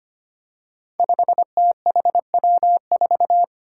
Morse code, heard as a keyed tone, about 25 wpm, 700 Hz. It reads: 5THW4
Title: Five Challenging Letter-Number Combinations 25wpm